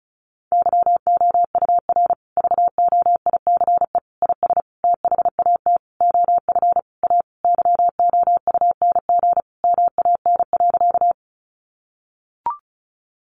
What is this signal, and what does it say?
Morse code, heard as a keyed tone, about 34 wpm, 700 Hz. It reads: YOUR VOICE IS THAT OF A YOUNG MAN.
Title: Sentences from Top 500 Words 35wpm